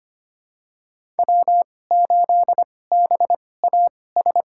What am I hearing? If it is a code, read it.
Morse code, heard as a keyed tone, about 25 wpm, spaced as for 18 wpm, 700 Hz. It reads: W8BAS
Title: USA Call Signs - 5 Characters - 2x Character Spacing - 25wpm